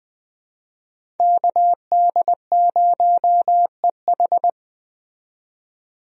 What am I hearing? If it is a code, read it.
Morse code, heard as a keyed tone, about 20 wpm, 700 Hz. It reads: KD0EH
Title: USA Call Signs - 5 Characters - 20wpm